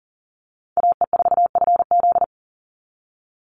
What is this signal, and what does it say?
Morse code, heard as a keyed tone, about 40 wpm, 700 Hz. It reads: AE4FZ
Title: NA SOTA Chaser Callsigns 40wpm